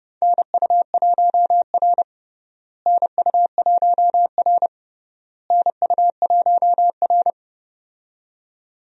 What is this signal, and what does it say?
Morse code, heard as a keyed tone, about 31 wpm, 700 Hz. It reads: NU1R NU1R NU1R
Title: USA Call Signs - 4 Characters - Repeated 3x - 30wpm